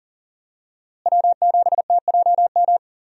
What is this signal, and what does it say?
Morse code, heard as a keyed tone, about 40 wpm, 700 Hz. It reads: W7TJM
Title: USA Call Signs - 5 Characters - Repeated 3x - 40wpm